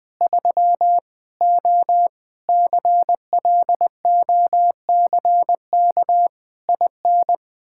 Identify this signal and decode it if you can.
Morse code, heard as a keyed tone, about 20 wpm, 700 Hz. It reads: 3 O CLOCK IN
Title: Pangram Challenge v2 - 20wpm